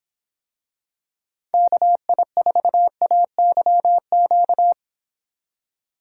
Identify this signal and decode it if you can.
Morse code, heard as a keyed tone, about 26 wpm, 700 Hz. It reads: KI4AYQ